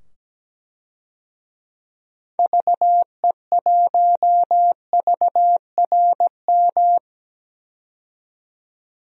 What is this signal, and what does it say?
Morse code, heard as a keyed tone, about 17 wpm, 700 Hz. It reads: VE1VRM
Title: DXCC-100 Call Signs - 6 Characters - 17wpm